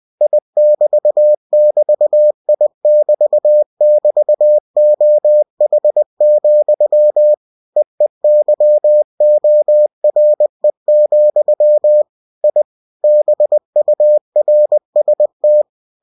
Morse code, heard as a keyed tone, about 20 wpm, 600 Hz. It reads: I<BT><BT>I<BT><BT>OH, EEYORE, I BURST
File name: cw-books / winnie-the-pooh / winnie-20wpm-page-0055